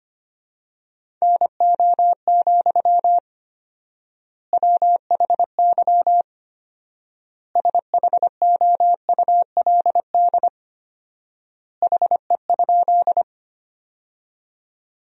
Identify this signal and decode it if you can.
Morse code, heard as a keyed tone, about 25 wpm, 700 Hz. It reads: NO, WHY SHOULD HE?
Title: 4x Word-Spacing - Sentences from Top 300 Words - 25wpm